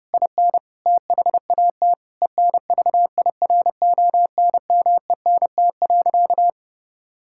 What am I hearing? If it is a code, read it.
Morse code, heard as a keyed tone, about 32 wpm, 700 Hz. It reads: IN THAT ENVIRONMENT.
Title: Sentences from Top 1000 Words - Review Part 5 - 30wpm